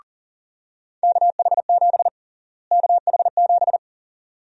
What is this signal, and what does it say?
Morse code, heard as a keyed tone, about 40 wpm, 700 Hz. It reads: KH7 KH7
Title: USA Call Sign Prefixes - Repeated 2x - 40wpm